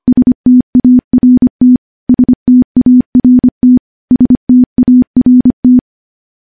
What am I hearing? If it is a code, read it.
Morse code, heard as a keyed tone, about 25 wpm, 250 Hz. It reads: START START START
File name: morse / morse-training-25WPM